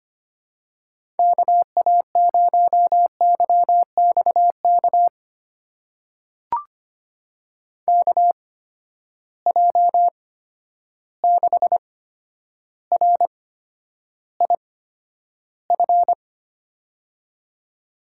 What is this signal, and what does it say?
Morse code, heard as a keyed tone, about 25 wpm, spaced as for 11 wpm, 700 Hz. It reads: KA0YXK KJ6RIF
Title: USA Call Signs - 6 Characters - 8x Character Spacing - 25wpm